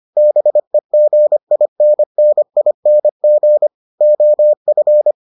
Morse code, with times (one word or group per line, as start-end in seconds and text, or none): 0.2-3.7 BEGINNING
4.0-5.1 OF